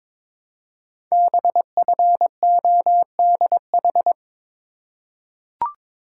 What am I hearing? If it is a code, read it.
Morse code, heard as a keyed tone, about 22 wpm, 700 Hz. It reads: BFODH
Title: Code Groups - 5 Letters - 22wpm